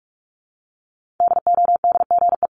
Morse code, heard as a keyed tone, about 45 wpm, 700 Hz. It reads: DODGE
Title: Multi-letter Phonemes Repeated 3x 45wpm